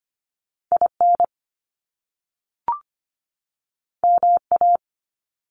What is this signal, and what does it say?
Morse code, heard as a keyed tone, about 25 wpm, 700 Hz. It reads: IN MA
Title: US States 25wpm